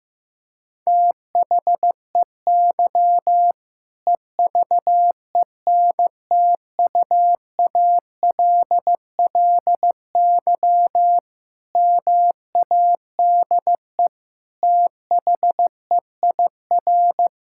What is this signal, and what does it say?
Morse code, heard as a keyed tone, about 15 wpm, 700 Hz. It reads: THEY EVENTUALLY MADE THEIR
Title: Sentences from Top 1000 Words - Review Part 6 - 15wpm